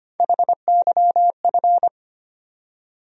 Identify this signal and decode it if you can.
Morse code, heard as a keyed tone, about 25 wpm, 700 Hz. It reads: HYF
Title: USA Call Sign Postfixes - 3 Letters 25wpm